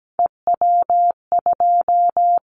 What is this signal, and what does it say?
Morse code, heard as a keyed tone, about 17 wpm, 700 Hz. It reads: EW2
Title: Callsign Prefixes for Top 100 DXCC - 17wpm